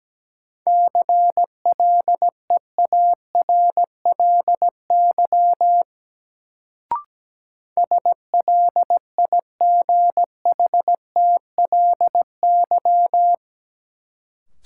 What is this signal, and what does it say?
Morse code, heard as a keyed tone, about 17 wpm, 700 Hz. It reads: CLEARLY SLIGHTLY
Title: Sentences from Top 1300 Words - New Word Drill Alt - 17wpm